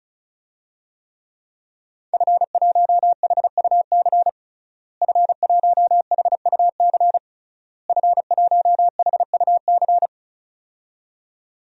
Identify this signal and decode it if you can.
Morse code, heard as a keyed tone, about 36 wpm, 700 Hz. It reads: F1HUC F1HUC F1HUC
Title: DXCC-100 Call Signs - 5 Characters - Repeated 3x - 35wpm